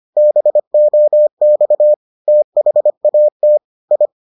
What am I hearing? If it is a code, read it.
Morse code, heard as a keyed tone, about 25 wpm, 600 Hz. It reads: BOX THAT I